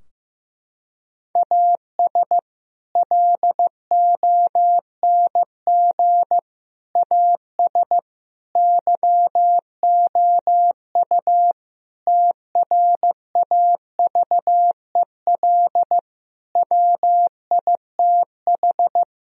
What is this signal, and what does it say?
Morse code, heard as a keyed tone, about 15 wpm, 700 Hz. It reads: AS LONG AS YOU TRAVEL WITH
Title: Sentences from Top 1000 Words - Review Part 2 - 15wpm